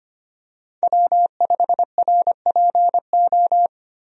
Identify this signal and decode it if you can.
Morse code, heard as a keyed tone, about 25 wpm, 700 Hz. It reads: W5RPO
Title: USA Call Signs - 5 Characters - 25wpm